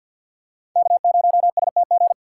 Morse code, heard as a keyed tone, about 52 wpm, 700 Hz. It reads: K0STG
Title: USA Call Signs - 5 Characters - 4x Character Spacing - 50wpm